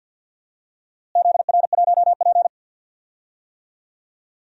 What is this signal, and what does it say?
Morse code, heard as a keyed tone, about 52 wpm, 700 Hz. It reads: ZR1P
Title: DXCC-100 Call Signs - 4 Characters - Repeated 3x - 50wpm